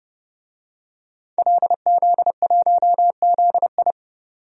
Morse code, heard as a keyed tone, about 32 wpm, 700 Hz. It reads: LZ1ZI
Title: DXCC-100 Call Signs - 5 Characters - 30wpm